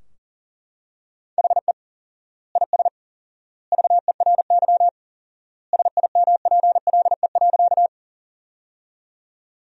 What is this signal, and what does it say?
Morse code, heard as a keyed tone, about 37 wpm, 700 Hz. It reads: HE IS VERY SIMPLE.
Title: 4x Word-Spacing - Sentences from Top 400 Words - 40wpm